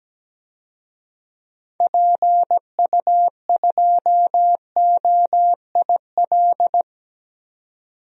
Morse code, heard as a keyed tone, about 17 wpm, 700 Hz. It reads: PU2OIL
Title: DXCC-100 Call Signs - 6 Characters - 17wpm